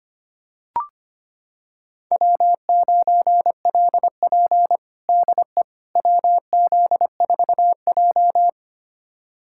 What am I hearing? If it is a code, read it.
Morse code, heard as a keyed tone, about 25 wpm, 700 Hz. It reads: W9LP DE WZ4J
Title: USA Callsign DE Callsign - 4 Characters - 25wpm